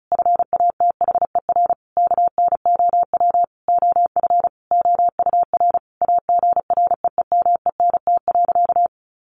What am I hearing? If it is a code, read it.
Morse code, heard as a keyed tone, about 34 wpm, 700 Hz. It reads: FATHER KNOW OF OUR AGREEMENT.
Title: Sentences from Top 1500 Words 35wpm